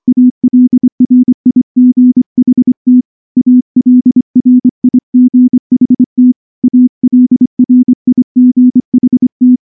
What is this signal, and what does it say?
Morse code, heard as a keyed tone, about 23 wpm, 250 Hz. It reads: ALRIGHT ALRIGHT ALRIGHT
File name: morse / morse-training-25WPM